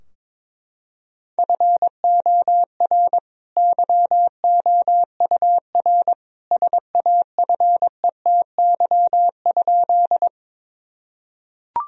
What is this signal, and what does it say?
Morse code, heard as a keyed tone, about 22 wpm, 700 Hz. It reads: FOR YOUR SAFETY?
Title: Sentences from Top 1000 Words - Review Part 4 - 22wpm